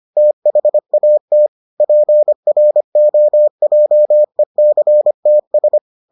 Morse code, heard as a keyed tone, about 25 wpm, 600 Hz. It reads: THAT PROJECTS